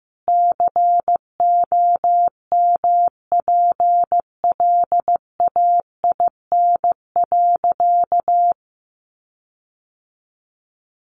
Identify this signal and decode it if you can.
Morse code, heard as a keyed tone, about 15 wpm, 700 Hz. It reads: COMPLAIN.
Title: Sentences from Top 1300 Words 15wpm